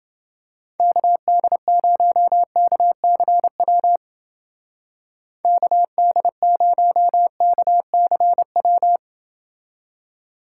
Morse code, heard as a keyed tone, about 29 wpm, 700 Hz. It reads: KD0KCW KD0KCW